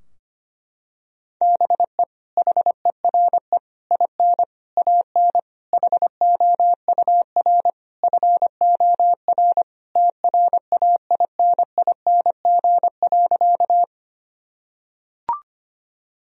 Morse code, heard as a keyed tone, about 25 wpm, 700 Hz. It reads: BE HERE IN AN HOUR FOR TRAINING.
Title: Sentences from Top 1000 Words - Review Part 4 - 25wpm